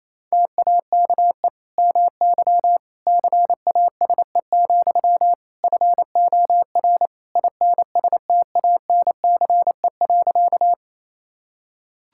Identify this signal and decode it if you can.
Morse code, heard as a keyed tone, about 28 wpm, 700 Hz. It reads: TAKE MY CASE, FOR INSTANCE.
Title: Sentences from Top 1000 Words - Review Part 5 - 28wpm